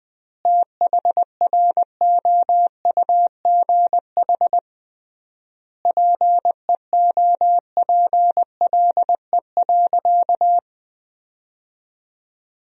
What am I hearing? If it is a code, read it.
Morse code, heard as a keyed tone, about 20 wpm, 700 Hz. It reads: THROUGH PEOPLE.
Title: 3x Word-Spacing - Sentences from Top 200 Words - 20wpm